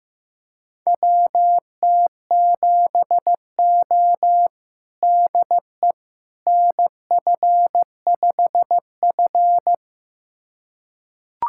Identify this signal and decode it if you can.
Morse code, heard as a keyed tone, about 15 wpm, 700 Hz. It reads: WT7O DE NF5F
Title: USA Callsign DE Callsign - 4 Characters - 15wpm